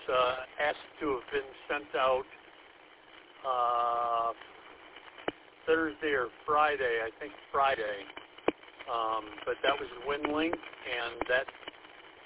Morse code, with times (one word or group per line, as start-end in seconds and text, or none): none